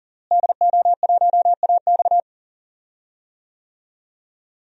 Morse code, heard as a keyed tone, about 40 wpm, 700 Hz. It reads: DO1AX
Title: DXCC-100 Call Signs - 5 Characters - Repeated 3x - 40wpm